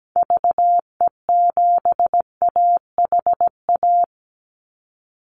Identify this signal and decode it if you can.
Morse code, heard as a keyed tone, about 17 wpm, 700 Hz. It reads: VE7AHA
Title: Call Signs Repeated 2x - 17wpm